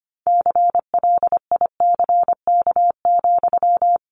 Morse code, heard as a keyed tone, about 25 wpm, 700 Hz. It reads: CLICK,
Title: Sentences from Top 1100 Words 25wpm